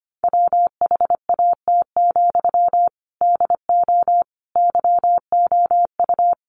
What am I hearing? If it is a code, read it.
Morse code, heard as a keyed tone, about 25 wpm, 700 Hz. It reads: WHAT, DO YOU
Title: Sentences from Top 600 Words 25wpm